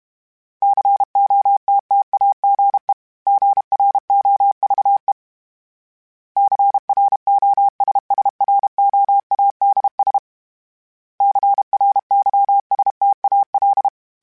Morse code, heard as a keyed tone, about 32 wpm, 800 Hz. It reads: COTTAGE GROVE CROSSROADS CRYSTAL
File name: fromSMs / FromSMs mp3s / Cities@30